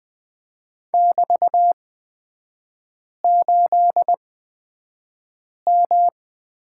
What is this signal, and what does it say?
Morse code, heard as a keyed tone, about 20 wpm, spaced as for 5 wpm, 700 Hz. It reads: <BT>8M
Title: 224 Lesson 36e - BT - Three Characters 20wpm